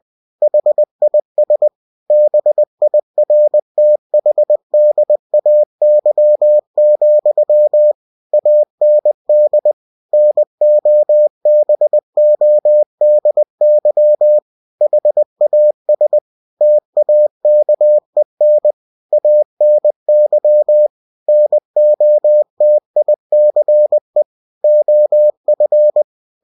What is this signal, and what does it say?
Morse code, heard as a keyed tone, about 20 wpm, 600 Hz. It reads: HIS BIRTHDAY, AND NOBODY HAS TAKEN ANY NOTICE OF
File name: cw-books / winnie-the-pooh / winnie-20wpm-page-0048